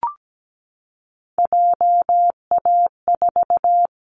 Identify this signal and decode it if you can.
Morse code, heard as a keyed tone, about 17 wpm, 700 Hz. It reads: JA4
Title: Callsign Prefixes for Top 50 DXCC - 17wpm